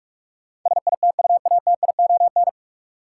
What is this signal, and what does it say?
Morse code, heard as a keyed tone, about 46 wpm, 700 Hz. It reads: SITUATION
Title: Sentences from Top 1200 Words - New Word Drill - 45wpm